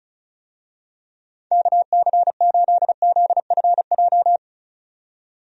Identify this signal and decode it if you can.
Morse code, heard as a keyed tone, about 36 wpm, 700 Hz. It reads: KC8ZFJ
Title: USA Call Signs - 6 Characters - 35wpm